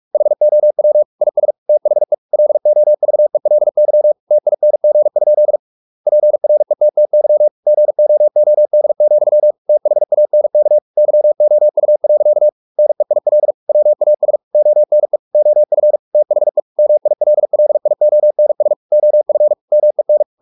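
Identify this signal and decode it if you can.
Morse code, heard as a keyed tone, about 44 wpm, 600 Hz. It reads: HOW IS THE LOVELY TING? PRETTY GOOD, THANK YOU. NEIL WAS ONE OF THE MILLIONS OF MEN
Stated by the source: Morse practice at 45 wpm, the novel Never by Ken Follett in English, from CW med Gnister.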